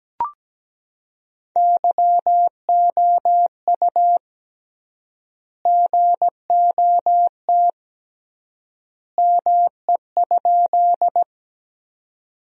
Morse code, heard as a keyed tone, about 17 wpm, 700 Hz. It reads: YOU GOT ME?
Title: 3x Word-Spacing - Sentences from Top 300 Words - 17wpm